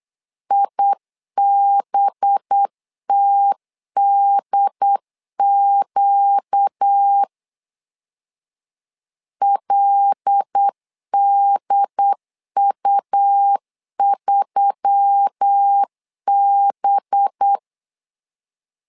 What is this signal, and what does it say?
Morse code, heard as a keyed tone, about 8 wpm, 800 Hz. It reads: IBTDQ LDU3B
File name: LEZIONI / 5 LBVC / 5 LB 8wpm 2816